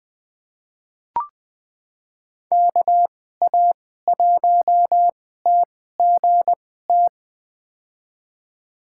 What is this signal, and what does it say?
Morse code, heard as a keyed tone, about 20 wpm, spaced as for 14 wpm, 700 Hz. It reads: KA1TGT
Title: USA Call Signs - 6 Characters - 2x Character Spacing - 20wpm